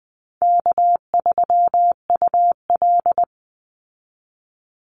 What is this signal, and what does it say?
Morse code, heard as a keyed tone, about 20 wpm, 700 Hz. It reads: K3UL
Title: Easy Call Signs Repeated 3x - 20wpm